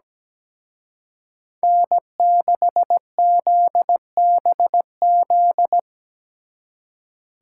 Morse code, heard as a keyed tone, about 20 wpm, 700 Hz. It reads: N6ZBZ
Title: Speed Racing - USA 1x3 Call Signs 17wpm